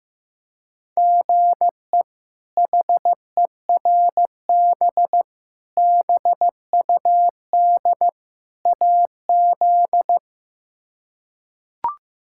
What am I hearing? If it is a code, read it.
Morse code, heard as a keyed tone, about 15 wpm, 700 Hz. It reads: GE HERB BUD AZ
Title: K1USN SST QSO Practice - 15wpm